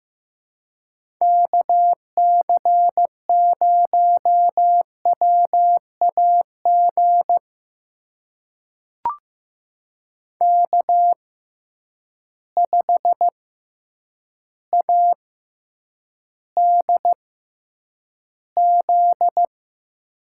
Morse code, heard as a keyed tone, about 15 wpm, spaced as for 8 wpm, 700 Hz. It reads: KC0WAG K5ADZ
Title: USA Call Signs - 6x Character Spacing - 15wpm